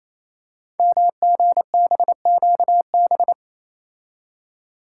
Morse code, heard as a keyed tone, about 27 wpm, 700 Hz. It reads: MGBQB
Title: Code Groups - 5 Letters - 28wpm